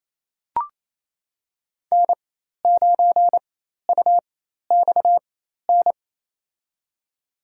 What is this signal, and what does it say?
Morse code, heard as a keyed tone, about 28 wpm, spaced as for 13 wpm, 690 Hz. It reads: N9UXN
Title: USA Call Signs - 5 Characters - 4x Character Spacing - 28wpm